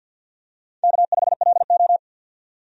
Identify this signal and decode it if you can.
Morse code, heard as a keyed tone, about 51 wpm, 700 Hz. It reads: K5LO
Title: USA Call Signs - 4 Characters - 50wpm